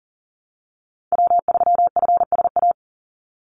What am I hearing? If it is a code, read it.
Morse code, heard as a keyed tone, about 40 wpm, 700 Hz. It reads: W3FSA
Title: Call Signs - 40wpm